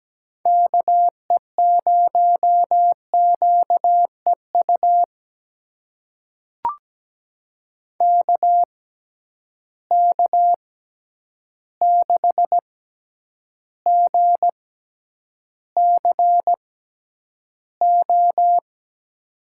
Morse code, heard as a keyed tone, about 17 wpm, spaced as for 9 wpm, 700 Hz. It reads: KE0QEU KK6GCO
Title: USA Call Signs - 6 Characters - 6x Character Spacing - 17wpm